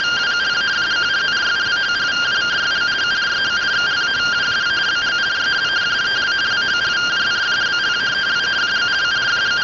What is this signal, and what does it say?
Two-tone teleprinter signal, about 170 Hz shift.
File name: findings / axial iodine 3mm 205.gen-25 rtty rx